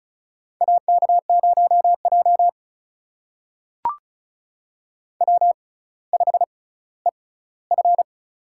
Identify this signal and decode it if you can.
Morse code, heard as a keyed tone, about 36 wpm, spaced as for 18 wpm, 700 Hz. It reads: AK0J W5EF
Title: USA Call Signs - 4 Characters - 6x Character Spacing - 35wpm